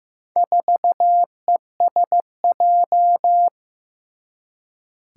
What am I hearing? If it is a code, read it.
Morse code, heard as a keyed tone, about 15 wpm, 700 Hz. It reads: 4ESJ
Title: Four Challenging Letter-Number Combinations 15wpm